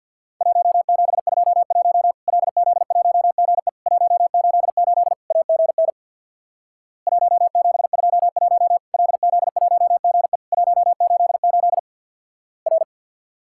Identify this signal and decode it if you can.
Morse code, heard as a keyed tone, about 50 wpm, 700 Hz. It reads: 1721 LZ1GE 188 AGN 1721 LZ1GE 188 R
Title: Worked All Europe DX Contest - QTC Exchange with Repeats - 50wpm